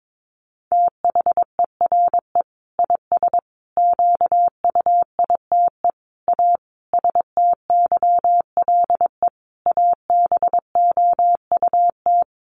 Morse code, heard as a keyed tone, about 22 wpm, 700 Hz. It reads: THERE IS QUITE A STYLE ABOUT
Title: Sentences from Top 800 Words 22wpm